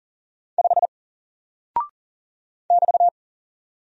Morse code, heard as a keyed tone, about 39 wpm, 700 Hz. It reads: 5 <BT>
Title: Single Character (Letter:Number:Punctuation:Prosign) Alt 40wpm